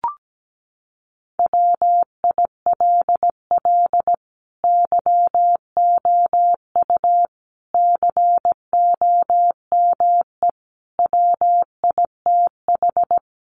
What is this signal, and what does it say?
Morse code, heard as a keyed tone, about 17 wpm, 700 Hz. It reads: WILL YOU COME WITH
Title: Sentences from Top 200 Words 17wpm